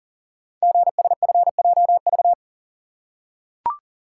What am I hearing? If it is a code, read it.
Morse code, heard as a keyed tone, about 38 wpm, 700 Hz. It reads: GSFJV
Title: Code Groups - 5 Letters - 40wpm